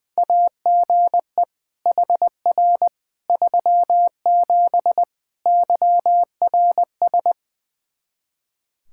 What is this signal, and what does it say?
Morse code, heard as a keyed tone, about 20 wpm, 700 Hz. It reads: AGE HR 37 YRS
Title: Practice QSO - Exchange 3 - More Info - 20wpm